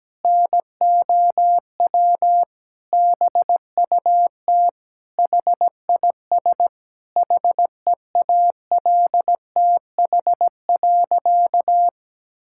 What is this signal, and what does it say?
Morse code, heard as a keyed tone, about 17 wpm, 700 Hz. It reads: NOW BUT HIS HEALTH.
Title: Sentences from Top 1000 Words - Review Part 6 - 17wpm